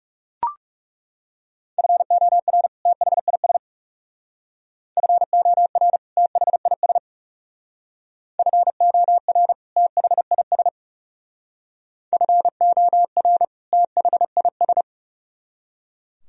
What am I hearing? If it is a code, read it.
Morse code, heard as a keyed tone, about 35 wpm, 700 Hz. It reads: FOR THIS FOR THIS FOR THIS FOR THIS
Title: Speed-Racing - Sets of 2 Words 30wpm